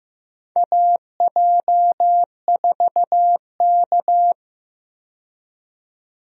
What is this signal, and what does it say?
Morse code, heard as a keyed tone, about 15 wpm, 700 Hz. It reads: AJ4K
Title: USA Call Signs - 4 Characters - 15wpm